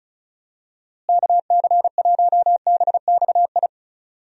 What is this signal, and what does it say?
Morse code, heard as a keyed tone, about 36 wpm, 700 Hz. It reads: KC1BXI
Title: USA Call Signs - 6 Characters - Repeated 2x - 35wpm